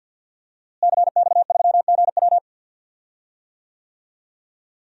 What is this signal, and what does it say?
Morse code, heard as a keyed tone, about 52 wpm, 700 Hz. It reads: CX3GW
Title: DXCC-100 Call Signs - 5 Characters - Repeated 2x - 50wpm